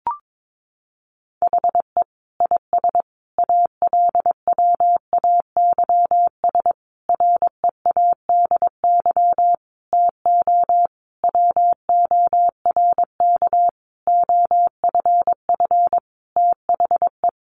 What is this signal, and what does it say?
Morse code, heard as a keyed tone, about 22 wpm, 700 Hz. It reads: HE IS ALWAYS READY TO WORK OFF THE
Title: Sentences from Top 1300 Words 22wpm